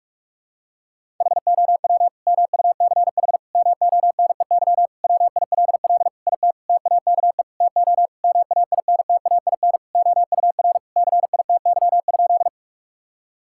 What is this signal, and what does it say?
Morse code, heard as a keyed tone, about 41 wpm, 700 Hz. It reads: HOW MUCH MONEY WILL IT TAKE TO MAINTAIN OUR CITY?
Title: Sentences from Top 1000 Words - Review Part 5 - 45wpm